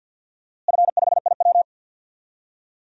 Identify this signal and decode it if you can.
Morse code, heard as a keyed tone, about 53 wpm, 700 Hz. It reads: F5IW
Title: DXCC-100 Call Signs - 4 Characters - Repeated 3x - 50wpm